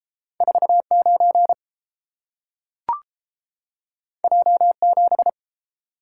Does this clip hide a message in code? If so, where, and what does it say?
Yes: Morse code from 0.4 to 5.3 s, heard as a keyed tone, about 29 wpm, 700 Hz. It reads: 49 J7